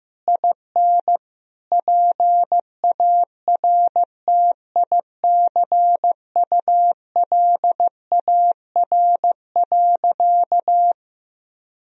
Morse code, heard as a keyed tone, about 15 wpm, 700 Hz. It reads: IN PARTICULAR.
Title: Sentences from Top 1000 Words - Review Part 4 - 15wpm